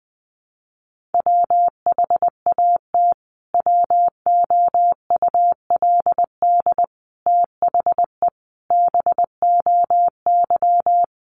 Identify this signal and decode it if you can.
Morse code, heard as a keyed tone, about 20 wpm, 700 Hz. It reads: WHAT WOULD THE BOY
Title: Sentences from Top 1300 Words 20wpm